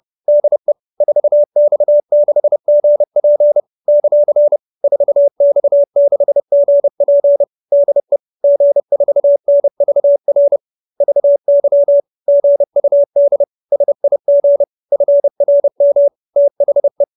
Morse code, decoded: DE 4X6GP ; 4X6GP DE G4NVR VY GUD SIG FRM THE

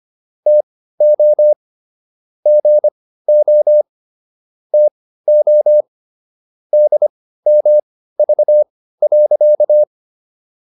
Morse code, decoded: TO GO TO DMV.

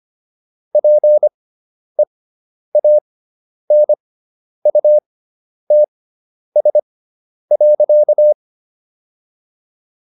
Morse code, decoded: PEANUTS.

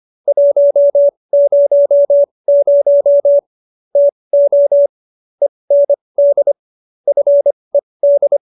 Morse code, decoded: 100 TO END FED

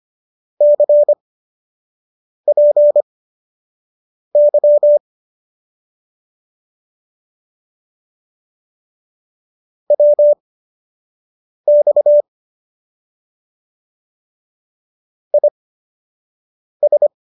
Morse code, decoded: CPY WX IS

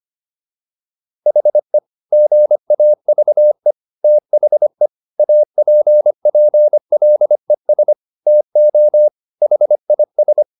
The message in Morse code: HE GAVE THE APPLES TO HIS

